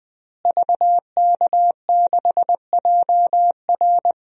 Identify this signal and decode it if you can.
Morse code, heard as a keyed tone, about 20 wpm, 700 Hz. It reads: VK6JR
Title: DXCC-100 Call Signs - 5 Characters - Repeated 2x - 20wpm